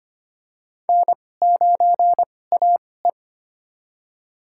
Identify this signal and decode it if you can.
Morse code, heard as a keyed tone, about 25 wpm, spaced as for 18 wpm, 700 Hz. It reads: N9AE
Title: USA Call Signs - 4 Characters - 2x Character Spacing - 25wpm